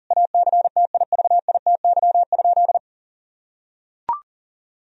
Morse code, heard as a keyed tone, about 40 wpm, 700 Hz. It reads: ACTIVITY?
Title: Sentences from Top 1000 Words - Review Part 4 - 40wpm